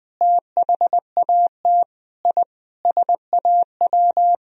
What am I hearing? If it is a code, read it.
Morse code, heard as a keyed tone, about 20 wpm, 700 Hz. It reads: THAT I SAW
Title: Sentences from Top 1000 Words - Review Part 1 - 20wpm